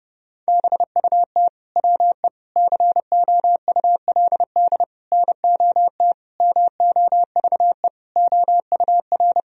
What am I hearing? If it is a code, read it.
Morse code, heard as a keyed tone, about 32 wpm, 700 Hz. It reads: BUT WE COULD NOT MOVE OUR
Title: Sentences from Top 1000 Words - Review Part 2 - 30wpm